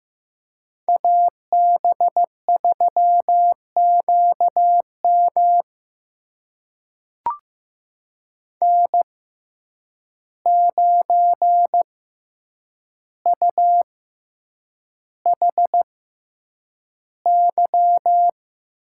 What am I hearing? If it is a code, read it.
Morse code, heard as a keyed tone, about 15 wpm, spaced as for 8 wpm, 700 Hz. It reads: AB3QM N9UHY
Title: USA Call Signs - 5 Characters - 6x Character Spacing - 15wpm